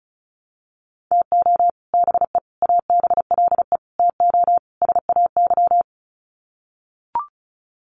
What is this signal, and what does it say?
Morse code, heard as a keyed tone, about 34 wpm, 700 Hz. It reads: TO BE ABLE TO SAY
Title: Sets of 5 Words Encore2 35wpm